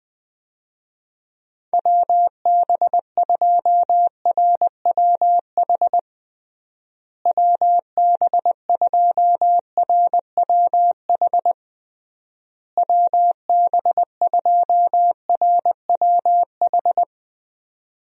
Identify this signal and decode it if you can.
Morse code, heard as a keyed tone, about 20 wpm, 700 Hz. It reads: WB2RWH WB2RWH WB2RWH
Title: USA Call Signs - 6 Characters - Repeated 3x - 20wpm